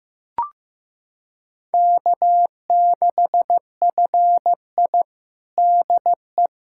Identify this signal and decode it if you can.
Morse code, heard as a keyed tone, about 15 wpm, 700 Hz. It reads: K6FI DE
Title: USA Callsign DE Callsign - 4 Characters - 15wpm